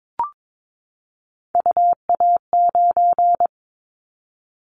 Morse code, heard as a keyed tone, about 22 wpm, 700 Hz. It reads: UA9
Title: Callsign Prefixes for Top 100 DXCC - 22wpm